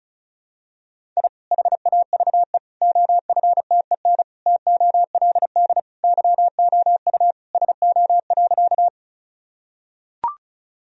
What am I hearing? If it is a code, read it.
Morse code, heard as a keyed tone, about 34 wpm, 700 Hz. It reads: I HAVE OFTEN TOLD YOU SO.
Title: Sentences from Top 1000 Words - Review Part 1 - 35wpm